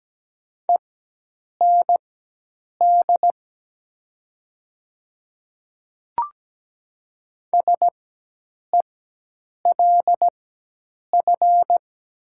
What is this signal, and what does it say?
Morse code, heard as a keyed tone, about 17 wpm, spaced as for 8 wpm, 700 Hz. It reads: END SELF
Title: 4x Character Spacing - Top 200 Words - 17wpm